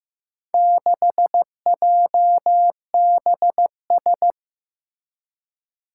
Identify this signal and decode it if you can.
Morse code, heard as a keyed tone, about 15 wpm, 700 Hz. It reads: 6JBS
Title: Four Challenging Letter-Number Combinations 15wpm